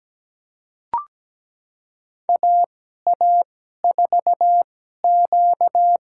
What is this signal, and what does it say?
Morse code, heard as a keyed tone, about 17 wpm, spaced as for 12 wpm, 700 Hz. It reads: AA4Q
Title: USA Call Signs - 2x Character Spacing - 17wpm